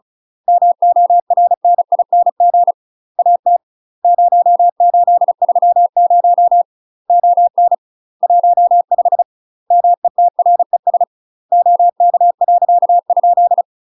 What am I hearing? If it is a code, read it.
Morse code, heard as a keyed tone, about 34 wpm, 700 Hz. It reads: MORNING AT 0830 ON 15 METRES OK . ?
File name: UK MASTER QSOs TRAINING / 35WPM / UK MASTER QSOs 0021